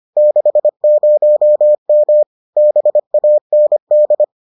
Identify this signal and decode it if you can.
Morse code, heard as a keyed tone, about 25 wpm, 600 Hz. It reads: 60M BAND